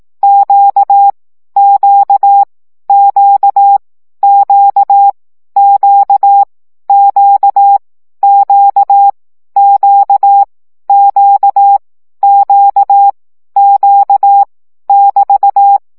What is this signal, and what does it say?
Morse code, heard as a keyed tone, about 18 wpm, 800 Hz. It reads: Q Q Q Q Q Q Q Q Q Q Q <BT>
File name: CW gyakorlas q 15 baud